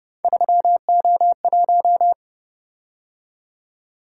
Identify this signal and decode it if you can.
Morse code, heard as a keyed tone, about 28 wpm, 700 Hz. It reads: 3O1